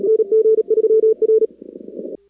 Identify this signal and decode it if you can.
Morse code, heard as a keyed tone, about 37 wpm, 440 Hz. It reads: NG3R
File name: contests / 2019-cw-wpx-kt5j-k5pi- -2330 0-4